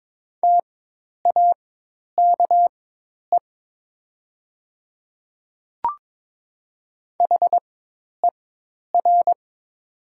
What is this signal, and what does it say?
Morse code, heard as a keyed tone, about 22 wpm, spaced as for 10 wpm, 700 Hz. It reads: TAKE HER
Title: 4x Character Spacing - Top 200 Words - 22wpm